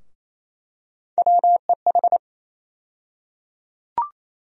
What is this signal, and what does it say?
Morse code, heard as a keyed tone, about 27 wpm, 700 Hz. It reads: WEH
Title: USA Call Sign Postfixes - 3 Letters 28wpm